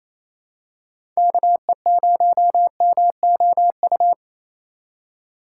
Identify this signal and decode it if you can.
Morse code, heard as a keyed tone, about 28 wpm, 700 Hz. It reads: KE0MOU
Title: USA Call Signs - 2x Character Spacing - 28wpm